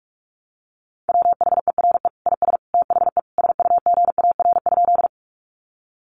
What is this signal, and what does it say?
Morse code, heard as a keyed tone, about 44 wpm, 700 Hz. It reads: WHERE IS THE SUGAR?
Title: Sentences from Top 1200 Words 45wpm